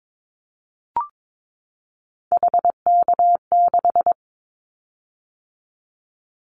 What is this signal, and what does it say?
Morse code, heard as a keyed tone, about 22 wpm, 700 Hz. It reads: HK6
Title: Callsign Prefixes for Top 100 DXCC - 22wpm